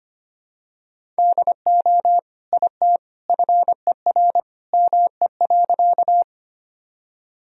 Morse code, heard as a keyed tone, about 25 wpm, 700 Hz. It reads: DO IT FER ME.
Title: Sentences from Top 200 Words and Abbreviations 25wpm